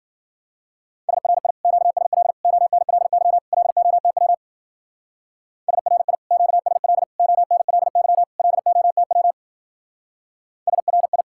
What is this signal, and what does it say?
Morse code, heard as a keyed tone, about 53 wpm, 700 Hz. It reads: SRI QSL ONLY LOTW SRI QSL ONLY LOTW SRI